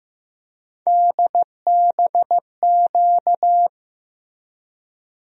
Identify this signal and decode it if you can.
Morse code, heard as a keyed tone, about 15 wpm, 700 Hz. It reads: DBQ
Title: USA Call Sign Postfixes - 3 Letters 15wpm